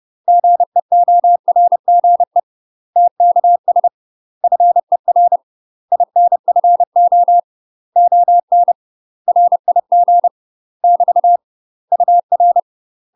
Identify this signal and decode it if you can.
Morse code, heard as a keyed tone, about 30 wpm, 700 Hz. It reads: GEORGE TKS FER INFO ON RIG <BT> UR
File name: UK MASTER QSOs TRAINING/30WPM